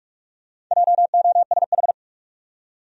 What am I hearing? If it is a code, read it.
Morse code, heard as a keyed tone, about 42 wpm, 700 Hz. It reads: JOSH
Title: USA Common Names - 45wpm